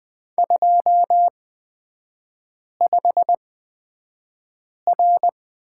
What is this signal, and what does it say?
Morse code, heard as a keyed tone, about 20 wpm, spaced as for 5 wpm, 700 Hz. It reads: 25R
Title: 255 Lesson 41e - Period - Three Characters 20wpm